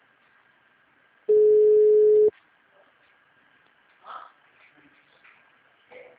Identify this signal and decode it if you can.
Ringback tone (Europe, 425 Hz).